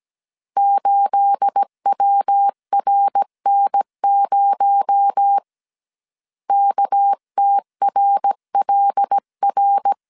Morse code, heard as a keyed tone, about 17 wpm, 800 Hz. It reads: 8WRN0 KTRLR
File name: 1500C S B 16wpm